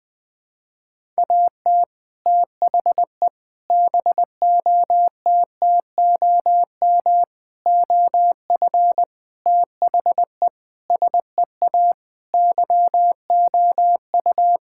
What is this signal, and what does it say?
Morse code, heard as a keyed tone, about 20 wpm, 700 Hz. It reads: AT THE BOTTOM OF THE SEA YOU
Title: Sentences with 3-Word Phrases - 20wpm